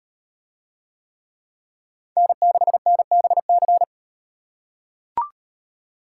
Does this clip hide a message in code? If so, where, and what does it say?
Yes: Morse code from 2.2 to 3.8 s, heard as a keyed tone, about 35 wpm, 700 Hz. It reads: N6NBC